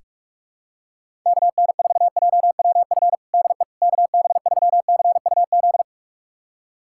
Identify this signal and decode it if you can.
Morse code, heard as a keyed tone, about 46 wpm, 700 Hz. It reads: KN4JWF DE KB3CUZ
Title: USA Callsign DE Callsign 45wpm